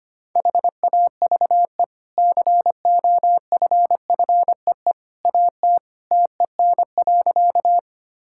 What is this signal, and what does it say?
Morse code, heard as a keyed tone, about 25 wpm, 700 Hz. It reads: HAVE COFFEE AT TEN.
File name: Easy Sentences with Hints - 25wpm